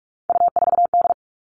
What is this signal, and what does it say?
Morse code, heard as a keyed tone, about 44 wpm, 700 Hz. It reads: U4D
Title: All Cut Numbers - 3-Digit 45wpm